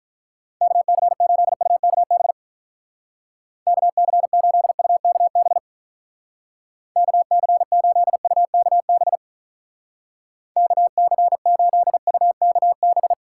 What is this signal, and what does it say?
Morse code, heard as a keyed tone, about 44 wpm, 700 Hz. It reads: KC8UKB KC8UKB KC8UKB KC8UKB